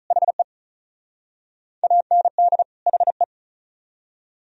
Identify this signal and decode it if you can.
Morse code, heard as a keyed tone, about 44 wpm, 700 Hz. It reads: HE AND HE
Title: Speed-Racing - Sets of 2 Words 35wpm